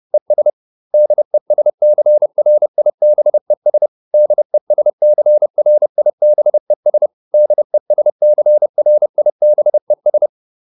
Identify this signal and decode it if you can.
Morse code, heard as a keyed tone, about 30 wpm, 600 Hz. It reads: ES DESCRIBES DESCRIBES DESCRIBES